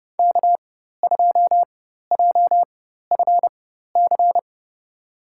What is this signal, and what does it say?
Morse code, heard as a keyed tone, about 30 wpm, spaced as for 14 wpm, 690 Hz. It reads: K2JFC
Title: USA Call Signs - 4x Character Spacing - 30wpm